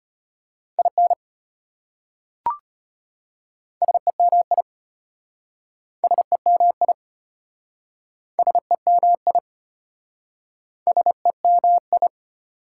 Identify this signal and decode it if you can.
Morse code, heard as a keyed tone, about 33 wpm, 700 Hz. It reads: IN SEMI SEMI SEMI SEMI